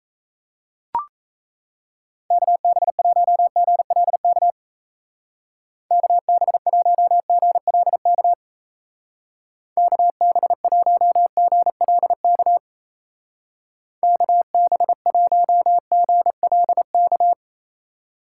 Morse code, heard as a keyed tone, about 34 wpm, 700 Hz. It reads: KB1GLK KB1GLK KB1GLK KB1GLK
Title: Speed Racing - USA 2x3 Call Signs 28wpm